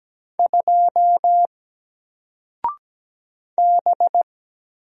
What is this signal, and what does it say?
Morse code, heard as a keyed tone, about 17 wpm, 700 Hz. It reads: 2 B
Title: Single Letter-Number 17wpm